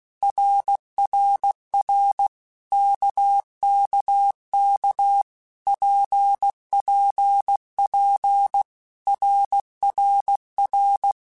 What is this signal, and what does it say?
Morse code, heard as a keyed tone, about 16 wpm, 800 Hz. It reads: RRR KKK PPP RRR
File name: Basic Letter Groups By7Groups / RKPX / RKPX@15